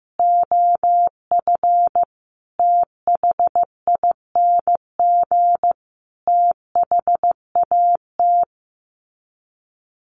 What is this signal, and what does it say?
Morse code, heard as a keyed tone, about 15 wpm, 700 Hz. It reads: OF THING THAT
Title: Sets of 5 Words Encore 15wpm